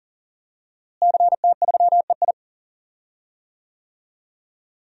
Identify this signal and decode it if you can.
Morse code, heard as a keyed tone, about 40 wpm, 700 Hz. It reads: CT3EI
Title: DXCC-100 Call Signs - 5 Characters - Repeated 2x - 40wpm